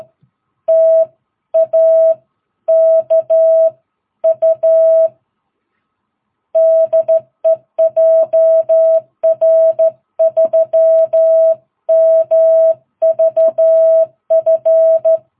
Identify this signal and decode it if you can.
Morse code, heard as a keyed tone, about 13 wpm, 660 Hz. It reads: TAKU DE JR3MVF
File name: jh8cbh / 2022cw